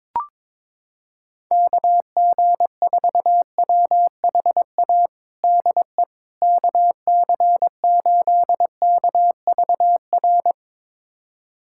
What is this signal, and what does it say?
Morse code, heard as a keyed tone, about 22 wpm, 700 Hz. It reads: KG4WHA DE KC8KVR
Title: USA Callsign DE Callsign - 6 Characters - 22wpm